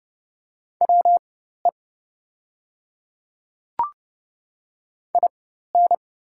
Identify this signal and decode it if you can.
Morse code, heard as a keyed tone, about 29 wpm, spaced as for 14 wpm, 690 Hz. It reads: WE IN
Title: 4x Character Spacing - Top 100 Words - 30wpm